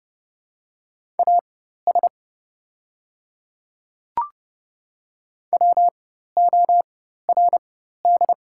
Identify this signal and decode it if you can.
Morse code, heard as a keyed tone, about 31 wpm, spaced as for 14 wpm, 700 Hz. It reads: AS WORD
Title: 4x Character Spacing - Top 100 Words - 30wpm